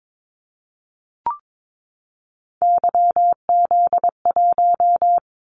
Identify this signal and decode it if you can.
Morse code, heard as a keyed tone, about 22 wpm, 700 Hz. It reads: YZ1
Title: Callsign Prefixes for Top 25 DXCC - 22wpm